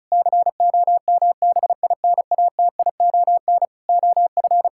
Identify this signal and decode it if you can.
Morse code, heard as a keyed tone, about 34 wpm, 700 Hz. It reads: COMBINATION OF
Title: Sentences from Top 1000 Words - Review Part 6 - 35wpm